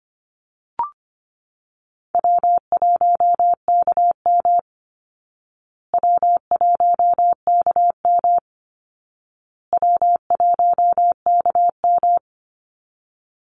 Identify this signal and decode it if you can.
Morse code, heard as a keyed tone, about 25 wpm, 700 Hz. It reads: W1KM W1KM W1KM
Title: Call Signs Repeated 2x - 25wpm